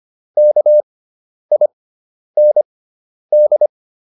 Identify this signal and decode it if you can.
Morse code, heard as a keyed tone, about 25 wpm, spaced as for 10 wpm, 600 Hz. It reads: KIND